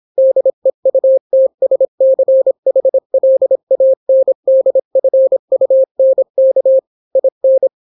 Morse code, heard as a keyed tone, about 26 wpm, 530 Hz. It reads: DEUTSCHLANDFUNK IN